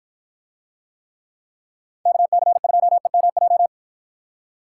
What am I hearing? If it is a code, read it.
Morse code, heard as a keyed tone, about 51 wpm, 700 Hz. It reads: KC2EMJ